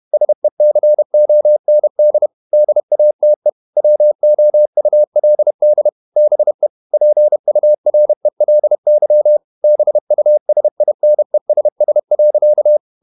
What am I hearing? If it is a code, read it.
Morse code, heard as a keyed tone, about 32 wpm, 600 Hz. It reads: SECOND DATE WOULD BE PURELY BUSINESS.
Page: Never 31wpm | CW med Gnister